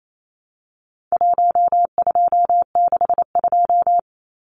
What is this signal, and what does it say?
Morse code, heard as a keyed tone, about 28 wpm, 700 Hz. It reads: 1262
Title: Numbers - 4-Digit 28wpm